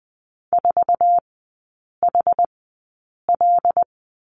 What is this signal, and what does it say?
Morse code, heard as a keyed tone, about 20 wpm, spaced as for 8 wpm, 700 Hz. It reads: 4HL
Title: 100 Lesson 17e - Letter u - Three Characters 20wpm